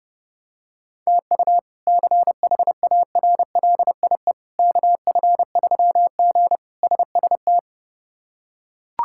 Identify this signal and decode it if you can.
Morse code, heard as a keyed tone, about 32 wpm, 700 Hz. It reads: TU CHARLIE KF3G SST
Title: K1USN SST QSO Practice - 30wpm